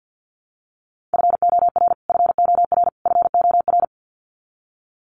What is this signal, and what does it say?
Morse code, heard as a keyed tone, about 50 wpm, 700 Hz. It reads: FOR FOR FOR
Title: Top 300 Words Repeated 50wpm